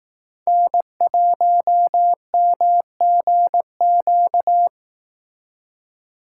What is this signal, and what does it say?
Morse code, heard as a keyed tone, about 18 wpm, 700 Hz. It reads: N1MGQ
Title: Speed Racing - USA 1x3 Call Signs 15wpm